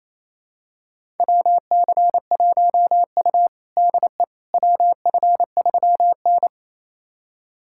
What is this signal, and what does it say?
Morse code, heard as a keyed tone, about 28 wpm, 700 Hz. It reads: WC1U DE WF3N
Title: USA Callsign DE Callsign - 4 Characters - 28wpm